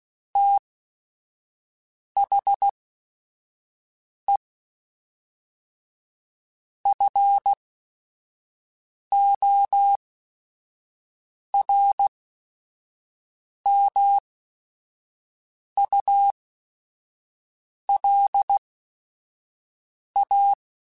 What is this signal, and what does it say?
Morse code, heard as a keyed tone, about 16 wpm, spaced as for 5 wpm, 800 Hz. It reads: THE FORMULA
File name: Technical Subjects / Technical Subjects-mp3s / XC @5